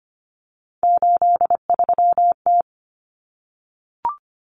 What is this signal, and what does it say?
Morse code, heard as a keyed tone, about 25 wpm, 700 Hz. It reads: 83T
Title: Common Cut Numbers - 3-Digit 25wpm